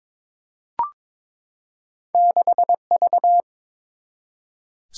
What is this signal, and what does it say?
Morse code, heard as a keyed tone, about 22 wpm, 700 Hz. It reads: 6V
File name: practice / 22wpm / Two Letter-Number Combinations - 22wpm